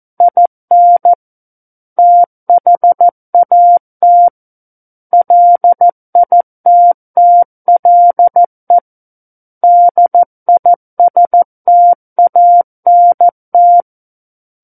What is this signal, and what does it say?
Morse code, heard as a keyed tone, about 14 wpm, 700 Hz. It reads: IN THAT LITTLE DISTANT
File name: Book One / 05-Chapter 5 - 14 WPM